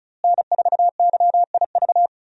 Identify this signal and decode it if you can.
Morse code, heard as a keyed tone, about 36 wpm, 700 Hz. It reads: N4YIV
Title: USA Call Signs - 5 Characters - Repeated 2x - 35wpm